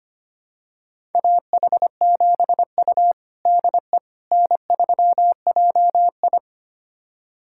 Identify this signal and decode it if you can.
Morse code, heard as a keyed tone, about 25 wpm, 700 Hz. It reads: AH7U DE N3JI
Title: USA Callsign DE Callsign - 4 Characters - 25wpm